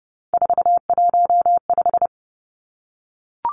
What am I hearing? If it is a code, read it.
Morse code, heard as a keyed tone, about 30 wpm, 700 Hz. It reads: 415
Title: Numbers - 3-Digit 30wpm